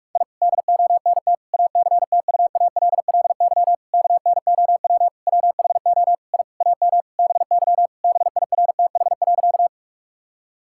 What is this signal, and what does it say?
Morse code, heard as a keyed tone, about 46 wpm, 700 Hz. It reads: I DONT ACTUALLY KNOW WHO I AM BY BIRTH.
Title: Sentences from Top 1000 Words - Review Part 5 - 45wpm